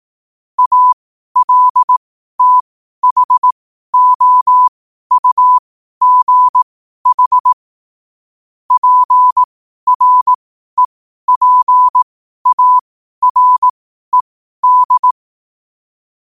Morse code, heard as a keyed tone, about 18 wpm, spaced as for 12 wpm, 1000 Hz. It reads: ALTHOUGH PREPARED
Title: 12 WPM morse code quotes for Fri, 01 Mar 2024